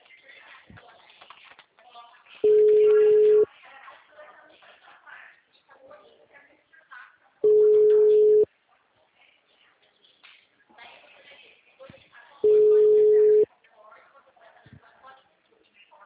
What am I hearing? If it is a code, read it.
Ringback tone (Europe, 425 Hz), with speech.